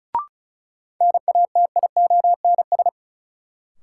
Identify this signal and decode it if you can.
Morse code, heard as a keyed tone, about 36 wpm, 700 Hz. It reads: NATIONS
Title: Sets of 3 Words - New Word Drill - 35wpm